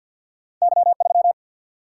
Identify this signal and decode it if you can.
Morse code, heard as a keyed tone, about 51 wpm, 700 Hz. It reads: Y3
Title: Single-Letter Single-Number 50wpm